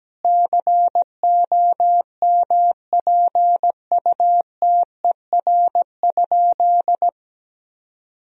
Morse code, heard as a keyed tone, about 17 wpm, 700 Hz. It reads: COMPUTER?
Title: Sentences from Top 1000 Words - Review Part 3 - 17wpm